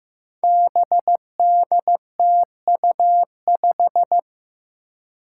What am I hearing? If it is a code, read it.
Morse code, heard as a keyed tone, about 15 wpm, 700 Hz. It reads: BDTU5
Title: Five Challenging Letter-Number Combinations 15wpm